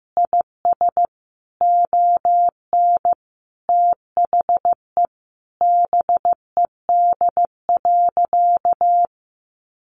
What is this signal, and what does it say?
Morse code, heard as a keyed tone, about 14 wpm, 700 Hz. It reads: IS ON THE BED.
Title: Sentences from Top 1400 Words 15wpm